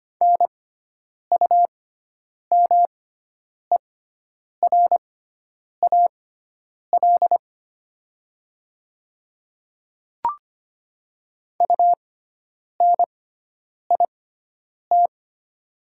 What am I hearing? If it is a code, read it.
Morse code, heard as a keyed tone, about 25 wpm, spaced as for 9 wpm, 700 Hz. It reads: NUMERAL UNIT
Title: 6x Character Spacing - Top 500 Words - 25wpm